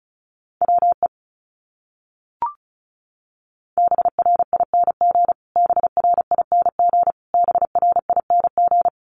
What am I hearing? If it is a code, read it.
Morse code, heard as a keyed tone, about 34 wpm, 700 Hz. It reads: WE BRING BRING BRING
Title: Top 500 Words Repeated 35wpm